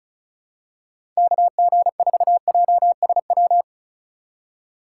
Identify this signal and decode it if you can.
Morse code, heard as a keyed tone, about 36 wpm, 700 Hz. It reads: KG4JSW
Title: USA Call Signs - 6 Characters - 3x Character Spacing - 35wpm